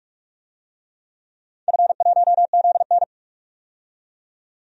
Morse code, heard as a keyed tone, about 42 wpm, 700 Hz. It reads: F1ZN
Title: DXCC-100 Call Signs - 4 Characters - 45wpm